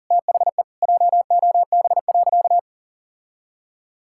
Morse code, heard as a keyed tone, about 40 wpm, 700 Hz. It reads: THE JOB.
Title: Sentences from Top 1600 Words 40wpm